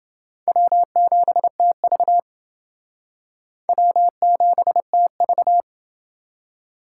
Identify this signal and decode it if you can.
Morse code, heard as a keyed tone, about 28 wpm, 700 Hz. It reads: W7TV W7TV